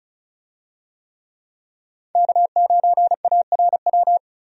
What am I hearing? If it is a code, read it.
Morse code, heard as a keyed tone, about 34 wpm, 700 Hz. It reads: K9ARW
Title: USA Call Signs - 5 Characters - 35wpm